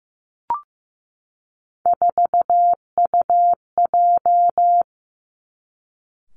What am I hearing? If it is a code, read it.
Morse code, heard as a keyed tone, about 15 wpm, 700 Hz. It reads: 4UJ
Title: Three Challenging Letter-Number Combinations 15wpm